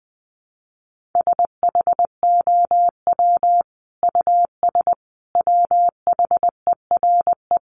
Morse code, heard as a keyed tone, about 20 wpm, 700 Hz. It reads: SHOW US WHERE
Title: Sentences from Top 200 Words 20wpm